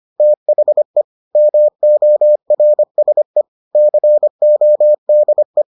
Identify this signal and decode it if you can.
Morse code, heard as a keyed tone, about 25 wpm, 600 Hz. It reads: THE MORSE CODE